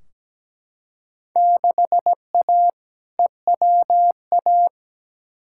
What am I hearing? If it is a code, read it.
Morse code, heard as a keyed tone, about 17 wpm, 700 Hz. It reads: 6A EWA
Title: Field Day Exchanges Encore - 17wpm